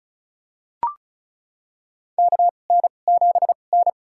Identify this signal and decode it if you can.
Morse code, heard as a keyed tone, about 34 wpm, spaced as for 25 wpm, 700 Hz. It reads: KN7N
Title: USA Call Signs - 4 Characters - 2x Character Spacing - 35wpm